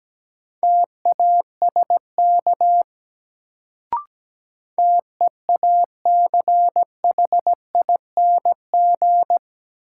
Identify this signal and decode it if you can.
Morse code, heard as a keyed tone, about 17 wpm, 700 Hz. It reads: TASK TEACHING
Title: Sentences from Top 1100 Words - New Word Drill Alt - 17wpm